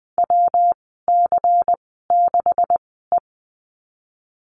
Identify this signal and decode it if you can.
Morse code, heard as a keyed tone, about 20 wpm, spaced as for 15 wpm, 700 Hz. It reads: WC6E
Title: 140 Lesson 23g - Letter f - Call Signs 20wpm